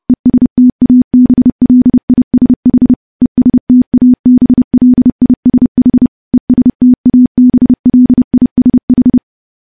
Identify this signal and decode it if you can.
Morse code, heard as a keyed tone, about 30 wpm, 250 Hz. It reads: ESTABLISH ESTABLISH ESTABLISH
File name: morse / morse-training-30WPM